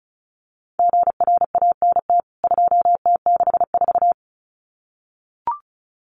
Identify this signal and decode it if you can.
Morse code, heard as a keyed tone, about 34 wpm, 700 Hz. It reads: GRANT 2T64
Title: CWT Exchange 35wpm